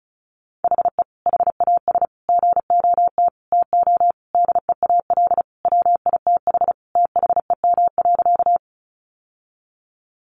Morse code, heard as a keyed tone, about 34 wpm, 700 Hz. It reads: HE HAS GOT TO DEAL WITH THEM.
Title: Sentences from Top 800 Words 35wpm